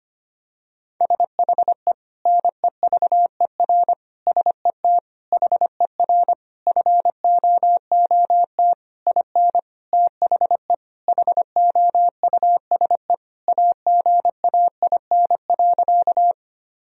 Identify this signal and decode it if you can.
Morse code, heard as a keyed tone, about 25 wpm, 700 Hz. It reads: SHE NEVER SET HER FOOT IN THE HOUSE AGAIN.
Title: Sentences from Top 1000 Words - Review Part 3 - 25wpm